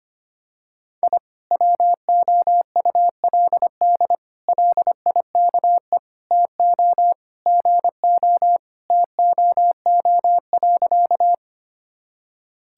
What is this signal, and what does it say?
Morse code, heard as a keyed tone, about 25 wpm, 700 Hz. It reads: I WOULD LIKE TO GO TOO.
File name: practice / 25wpm / Sentences with 3-Word Phrases - 25wpm